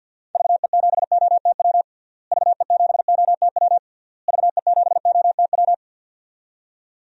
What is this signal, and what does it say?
Morse code, heard as a keyed tone, about 51 wpm, 700 Hz. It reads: VE7OTW VE7OTW VE7OTW
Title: DXCC-100 Call Signs - 6 Characters - Repeated 3x - 50wpm